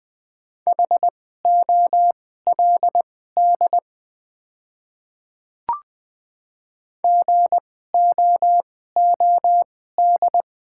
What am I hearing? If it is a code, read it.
Morse code, heard as a keyed tone, about 20 wpm, spaced as for 14 wpm, 690 Hz. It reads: HOLD GOOD
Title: 2x Character Spacing - Top 500 Words - 20wpm